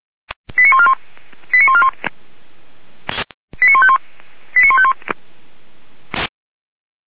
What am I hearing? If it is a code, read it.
Five-tone selective call (ZVEI), heard as quick runs of five tones, tones of about 70 ms. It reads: 89141 89141 89151 89151